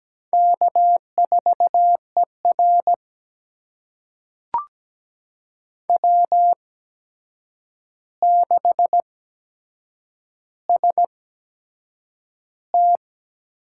Morse code, heard as a keyed tone, about 17 wpm, spaced as for 7 wpm, 700 Hz. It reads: K4ER W6ST
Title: USA Call Signs - 4 Characters - 8x Character Spacing - 17wpm